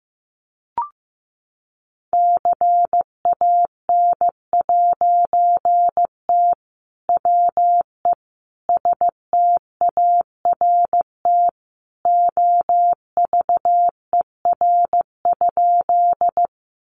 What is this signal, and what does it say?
Morse code, heard as a keyed tone, about 15 wpm, 700 Hz. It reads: CAN'T WE START OVER?
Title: A Sound Of Thunder 15wpm